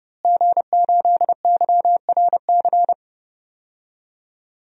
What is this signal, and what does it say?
Morse code, heard as a keyed tone, about 30 wpm, 700 Hz. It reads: G8YRC
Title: DXCC-100 Call Signs - 5 Characters - Repeated 2x - 30wpm